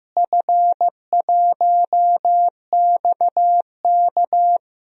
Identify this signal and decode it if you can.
Morse code, heard as a keyed tone, about 15 wpm, 700 Hz. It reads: F1XK
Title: DXCC-100 Call Signs - 4 Characters - Repeated 2x - 15wpm